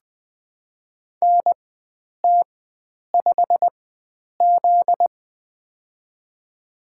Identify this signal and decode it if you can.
Morse code, heard as a keyed tone, about 20 wpm, spaced as for 9 wpm, 700 Hz. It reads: NT5Z
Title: USA Call Signs - 4 Characters - 4x Character Spacing - 20wpm